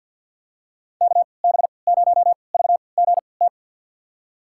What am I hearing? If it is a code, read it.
Morse code, heard as a keyed tone, about 46 wpm, spaced as for 27 wpm, 690 Hz. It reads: KB0VGT
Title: USA Call Signs - 6 Characters - 3x Character Spacing - 50wpm